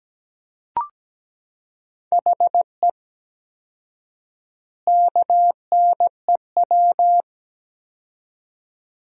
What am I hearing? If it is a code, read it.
Morse code, heard as a keyed tone, about 17 wpm, 700 Hz. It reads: HE KNEW
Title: 4x Word-Spacing - Sentences from Top 400 Words - 17wpm